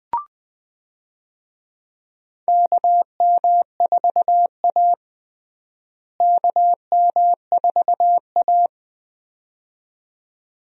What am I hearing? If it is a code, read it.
Morse code, heard as a keyed tone, about 20 wpm, 700 Hz. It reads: KM4A KM4A
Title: USA Call Signs - 4 Characters - Repeated 2x - 20wpm